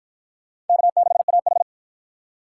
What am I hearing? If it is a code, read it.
Morse code, heard as a keyed tone, about 53 wpm, 700 Hz. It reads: K6AH